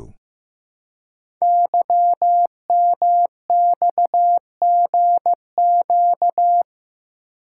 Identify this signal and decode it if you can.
Morse code, heard as a keyed tone, about 15 wpm, 700 Hz. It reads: YMXGQ
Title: Code Groups - 5 Letters - 15wpm